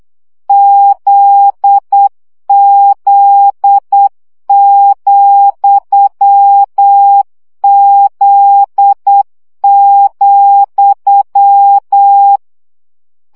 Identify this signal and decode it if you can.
Morse code, heard as a keyed tone, about 8 wpm, 800 Hz. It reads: ZZ,Z,
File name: CW gyakorlas z 7 vesszo 7 baud